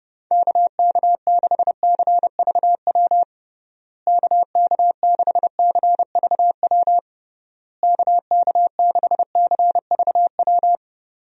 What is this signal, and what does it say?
Morse code, heard as a keyed tone, about 32 wpm, 700 Hz. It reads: KK6CVW KK6CVW KK6CVW
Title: USA Call Signs - 6 Characters - Repeated 3x - 30wpm